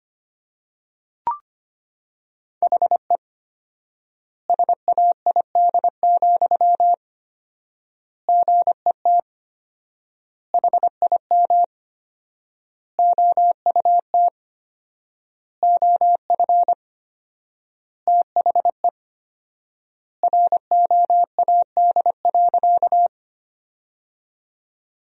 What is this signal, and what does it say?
Morse code, heard as a keyed tone, about 25 wpm, 700 Hz. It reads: HE SAID, GET HIM OUT OF THE ROAD.
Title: 4x Word-Spacing - Sentences from Top 400 Words - 25wpm